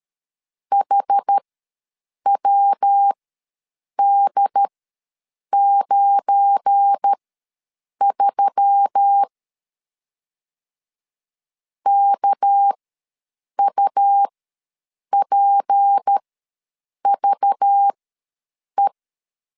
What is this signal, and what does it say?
Morse code, heard as a keyed tone, about 13 wpm, spaced as for 7 wpm, 800 Hz. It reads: HWD93 KUPVE
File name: LEZIONI / 9 28 ws 12wpm 3354